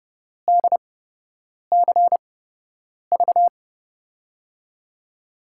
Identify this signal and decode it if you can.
Morse code, heard as a keyed tone, about 30 wpm, spaced as for 8 wpm, 700 Hz. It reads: DCV
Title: USA Call Signs - 5 Characters - 8x Character Spacing - 30wpm